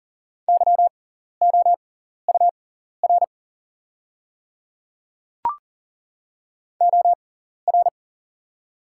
Morse code, heard as a keyed tone, about 40 wpm, spaced as for 14 wpm, 690 Hz. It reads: YOUR OR
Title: 6x Character Spacing - Top 100 Words - 40wpm